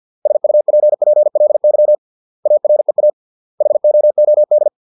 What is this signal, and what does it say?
Morse code, heard as a keyed tone, about 50 wpm, 600 Hz. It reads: SUPPLY AREA HOOD